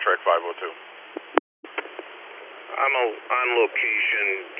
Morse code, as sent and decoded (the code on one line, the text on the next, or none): none